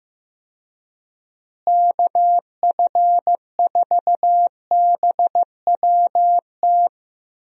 Morse code, heard as a keyed tone, about 15 wpm, 700 Hz. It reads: KF4BWT
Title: USA Call Signs - 6 Characters - Repeated 3x - 15wpm